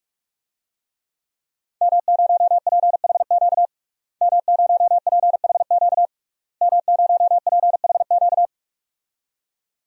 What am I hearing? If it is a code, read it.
Morse code, heard as a keyed tone, about 46 wpm, 700 Hz. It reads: M0PHQ M0PHQ M0PHQ
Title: DXCC-100 Call Signs - 5 Characters - Repeated 3x - 45wpm